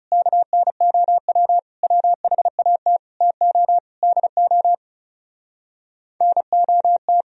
Morse code, heard as a keyed tone, about 37 wpm, 700 Hz. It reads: KNOW WHAT TO DO NOT
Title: Speed-Racing - Sets of 5 Words 30wpm